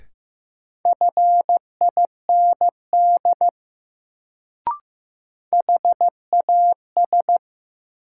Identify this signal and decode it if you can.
Morse code, heard as a keyed tone, about 15 wpm, 700 Hz. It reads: FIND HAS
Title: Sentences from Top 100 Words - New Word Drill - 15wpm